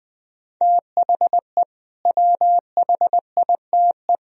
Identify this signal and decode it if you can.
Morse code, heard as a keyed tone, about 20 wpm, 700 Hz. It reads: THE WHITE
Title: Sentences from Top 1000 Words - Review Part 1 - 20wpm